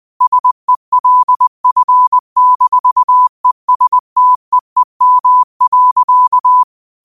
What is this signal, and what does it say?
Morse code, heard as a keyed tone, about 20 wpm, 1000 Hz. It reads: SELF-ESTEEM.
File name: qotdfiles / 20wpm